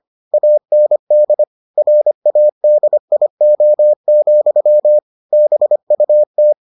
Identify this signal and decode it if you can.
Morse code, heard as a keyed tone, about 25 wpm, 600 Hz. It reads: AND RADIO, BUT